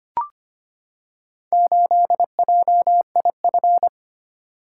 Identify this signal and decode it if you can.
Morse code, heard as a keyed tone, about 25 wpm, 700 Hz. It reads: 8JIF
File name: practice / 25wpm / Four Letter-Number Combinations - 25wpm